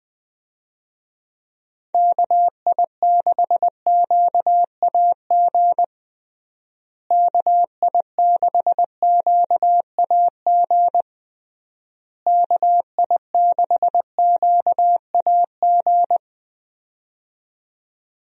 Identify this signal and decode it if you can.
Morse code, heard as a keyed tone, about 21 wpm, 700 Hz. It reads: KI6QAG KI6QAG KI6QAG
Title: USA Call Signs - 6 Characters - Repeated 3x - 20wpm